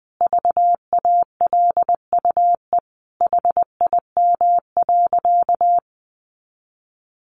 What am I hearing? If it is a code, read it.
Morse code, heard as a keyed tone, about 19 wpm, 700 Hz. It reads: VALUE HIM.
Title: Sentences from Top 600 Words 20wpm